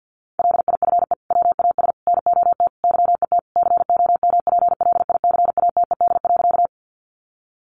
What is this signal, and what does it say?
Morse code, heard as a keyed tone, about 50 wpm, 700 Hz. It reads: LIFE WAS NOT YET COMPLICATED.
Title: Sentences from Top 1200 Words 50wpm